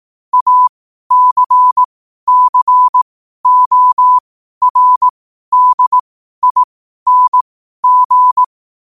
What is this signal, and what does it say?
Morse code, heard as a keyed tone, about 18 wpm, spaced as for 12 wpm, 1000 Hz. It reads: ACCORDING
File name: qotdfiles / 12wpm